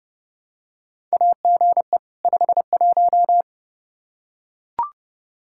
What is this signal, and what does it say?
Morse code, heard as a keyed tone, about 30 wpm, 700 Hz. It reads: AGE 51
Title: Practice QSO - Exchange 3 - More Info - 30wpm